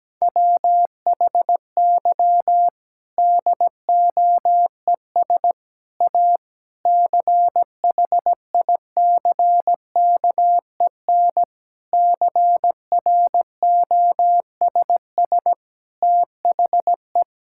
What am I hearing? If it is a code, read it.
Morse code, heard as a keyed tone, about 17 wpm, 700 Hz. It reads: WHY DOES A CHICKEN CROSS THE
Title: Sentences from Top 1000 Words - Review Part 5 - 17wpm